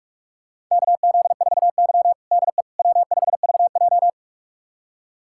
Morse code, heard as a keyed tone, about 46 wpm, 700 Hz. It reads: KZ4Y DE W5VJ
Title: USA Callsign DE Callsign - 4 Characters - 45wpm